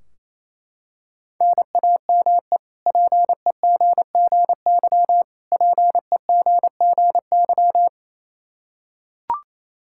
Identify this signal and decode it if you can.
Morse code, heard as a keyed tone, about 28 wpm, 700 Hz. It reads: NAME PEGGY PEGGY
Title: K1USN SST QSO Practice 28wpm